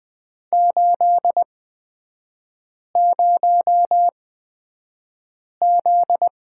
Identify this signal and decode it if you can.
Morse code, heard as a keyed tone, about 20 wpm, spaced as for 5 wpm, 700 Hz. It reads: 80Z
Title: 243 Lesson 39e - Letter z - Three Characters 20wpm